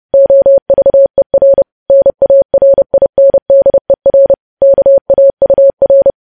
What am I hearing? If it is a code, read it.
Morse code, heard as a keyed tone, about 30 wpm, 550 Hz. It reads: OVER NARINDER KAUR
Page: news headlines in morse code at 30 wpm